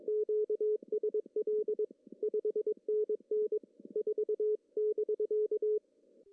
Morse code, heard as a keyed tone, about 22 wpm, 430 Hz. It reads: QSL 5NN 4<BK>
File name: CW / cqww-cw-k5tr